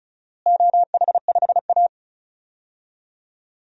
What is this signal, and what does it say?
Morse code, heard as a keyed tone, about 36 wpm, 700 Hz. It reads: OH5A
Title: DXCC-100 Call Signs - 4 Characters - Repeated 3x - 35wpm